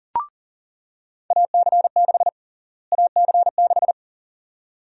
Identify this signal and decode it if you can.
Morse code, heard as a keyed tone, about 40 wpm, 700 Hz. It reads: AC6 AC6
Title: USA Call Sign Prefixes - Repeated 2x - 40wpm